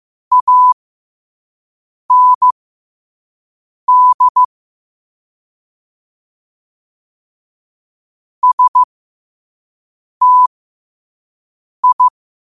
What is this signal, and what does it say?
Morse code, heard as a keyed tone, about 15 wpm, spaced as for 5 wpm, 1000 Hz. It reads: AND STI
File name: qotdfiles / 5wpm